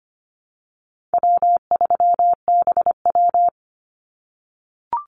Morse code, heard as a keyed tone, about 25 wpm, 700 Hz. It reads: W3BW
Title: Easy Call Signs - 25wpm